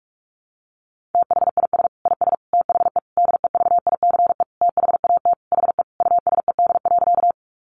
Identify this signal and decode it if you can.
Morse code, heard as a keyed tone, about 44 wpm, 700 Hz. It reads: THIS IS THE DEVICE THAT HE USED.
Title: Sentences from Top 1500 Words 45wpm